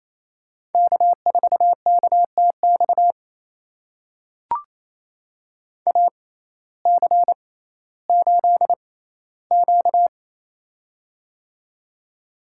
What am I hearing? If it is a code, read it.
Morse code, heard as a keyed tone, about 28 wpm, spaced as for 15 wpm, 700 Hz. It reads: K4KTX AC8Q
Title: USA Call Signs - 6x Character Spacing - 28wpm